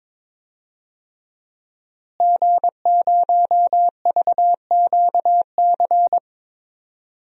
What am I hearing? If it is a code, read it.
Morse code, heard as a keyed tone, about 22 wpm, 700 Hz. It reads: G0VQC
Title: DXCC-100 Call Signs - 5 Characters - Repeated 2x - 22wpm